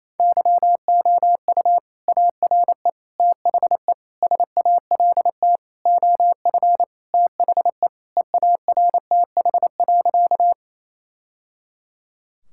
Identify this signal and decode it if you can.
Morse code, heard as a keyed tone, about 28 wpm, 700 Hz. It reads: YOU ARE THE SALT OF THE EARTH.
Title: Sentences from Top 1000 Words - Review Part 5 - 28wpm